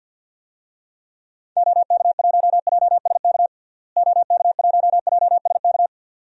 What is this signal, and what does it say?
Morse code, heard as a keyed tone, about 47 wpm, 700 Hz. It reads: OK1JSK OK1JSK
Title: DXCC-100 Call Signs - 6 Characters - Repeated 2x - 50wpm